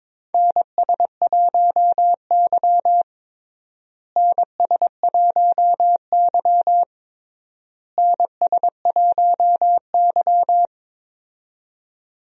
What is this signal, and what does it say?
Morse code, heard as a keyed tone, about 22 wpm, 700 Hz. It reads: NS1Y NS1Y NS1Y
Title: USA Call Signs - 4 Characters - Repeated 3x - 22wpm